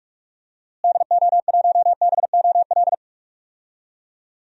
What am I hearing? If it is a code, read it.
Morse code, heard as a keyed tone, about 48 wpm, 700 Hz. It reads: DO1BOL
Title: DXCC-100 Call Signs - 6 Characters - Repeated 2x - 45wpm